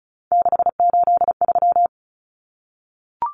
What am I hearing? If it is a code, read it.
Morse code, heard as a keyed tone, about 34 wpm, 700 Hz. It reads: 683
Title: Common Cut Numbers - 3-Digit 35wpm